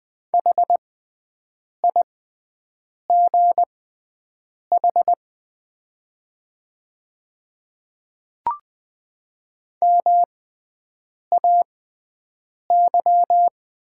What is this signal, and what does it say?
Morse code, heard as a keyed tone, about 20 wpm, spaced as for 7 wpm, 700 Hz. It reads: HIGH MAY
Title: 6x Character Spacing - Top 200 Words - 20wpm